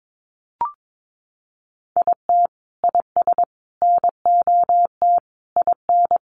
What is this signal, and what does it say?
Morse code, heard as a keyed tone, about 22 wpm, 700 Hz. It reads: IT IS NOT IN
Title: Sentences from Top 1400 Words 22wpm